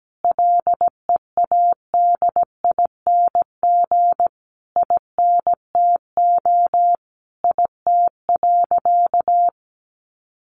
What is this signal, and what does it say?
Morse code, heard as a keyed tone, about 17 wpm, 700 Hz. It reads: LEADING INTO IT.
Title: Sentences from Top 1100 Words 17wpm